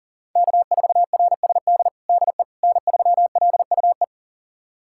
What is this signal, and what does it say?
Morse code, heard as a keyed tone, about 40 wpm, 700 Hz. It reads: K4RSD DE N3LUE
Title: USA Callsign DE Callsign - 5 Characters - 40wpm